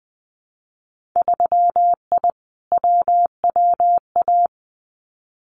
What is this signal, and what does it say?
Morse code, heard as a keyed tone, about 20 wpm, 700 Hz. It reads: 3I WWA
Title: Winter Field Day Exchanges Encore 20wpm